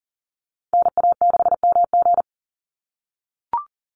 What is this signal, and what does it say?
Morse code, heard as a keyed tone, about 42 wpm, 700 Hz. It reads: NA6MG
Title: NA SOTA Chaser Callsigns 40wpm